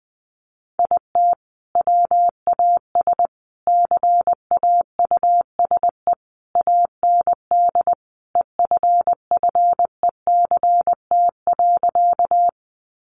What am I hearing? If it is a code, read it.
Morse code, heard as a keyed tone, about 20 wpm, 700 Hz. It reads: IT WAS CAUSE AND EFFECT.
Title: Sentences from Top 1000 Words 20wpm